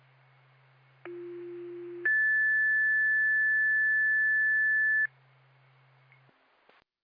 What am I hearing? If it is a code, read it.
Two-tone page, heard as a one-second tone, then a three-second tone, about 350 then 1750 Hz. It is heard from 1.1 to 5.1 s.